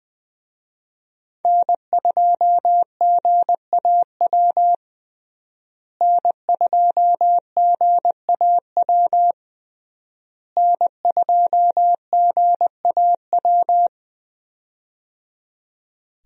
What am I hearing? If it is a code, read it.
Morse code, heard as a keyed tone, about 19 wpm, 700 Hz. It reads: N2GAW N2GAW N2GAW
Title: USA Call Signs - 5 Characters - Repeated 3x - 20wpm